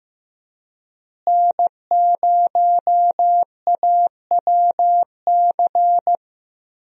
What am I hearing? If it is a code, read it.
Morse code, heard as a keyed tone, about 14 wpm, 700 Hz. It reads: N0AWC
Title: USA Call Signs - 5 Characters - Repeated 2x - 15wpm